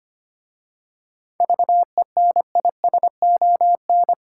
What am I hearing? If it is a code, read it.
Morse code, heard as a keyed tone, about 25 wpm, 700 Hz. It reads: VENISON
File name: practice / 25wpm / Difficult Words with Hints - 25wpm